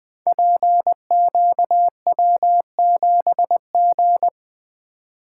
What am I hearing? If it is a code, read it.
Morse code, heard as a keyed tone, about 20 wpm, 700 Hz. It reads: PQW7G
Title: Code Groups - 5 Letter:Numbers - 20wpm